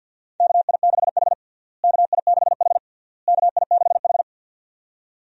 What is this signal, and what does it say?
Morse code, heard as a keyed tone, about 46 wpm, 700 Hz. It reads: KI6H KI6H KI6H
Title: USA Call Signs - 4 Characters - Repeated 3x - 50wpm